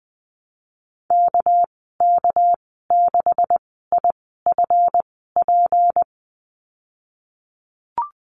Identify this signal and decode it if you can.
Morse code, heard as a keyed tone, about 20 wpm, spaced as for 14 wpm, 700 Hz. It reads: KK6IFP
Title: 207 Lesson 33g - Letter j - Call Signs 20wpm